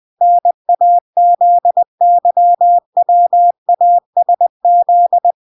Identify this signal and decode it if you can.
Morse code, heard as a keyed tone, about 20 wpm, 700 Hz. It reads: NAZYWASZ